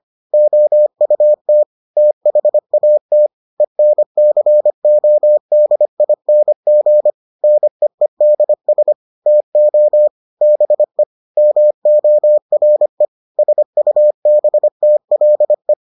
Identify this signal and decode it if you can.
Morse code, heard as a keyed tone, about 25 wpm, 600 Hz. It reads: OUT THAT ENCODING NEEDS TO BE MORE SUBTLE